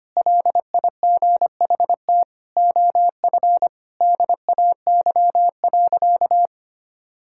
Morse code, heard as a keyed tone, about 25 wpm, 700 Hz. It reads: LIGHT OF DAY.
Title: Sentences from Top 1000 Words - Review Part 5 - 25wpm